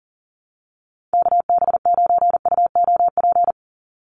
Morse code, heard as a keyed tone, about 40 wpm, 700 Hz. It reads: KB9UOP
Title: USA Call Signs 40wpm